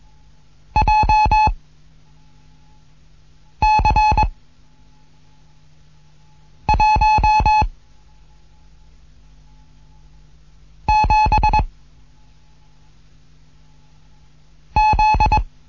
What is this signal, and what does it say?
Morse code, heard as a keyed tone, about 21 wpm, 860 Hz. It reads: J C 1 7 Z
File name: Morsen / Kurs A 15